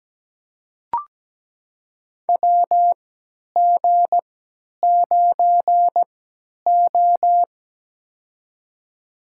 Morse code, heard as a keyed tone, about 17 wpm, spaced as for 10 wpm, 700 Hz. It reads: WG9O
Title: USA Call Signs - 3x Character Spacing - 17wpm